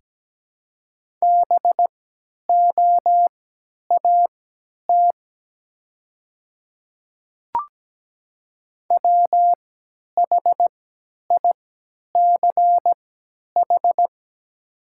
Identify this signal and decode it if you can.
Morse code, heard as a keyed tone, about 17 wpm, spaced as for 10 wpm, 700 Hz. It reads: BOAT WHICH
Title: 3x Character Spacing - Top 500 Words - 17wpm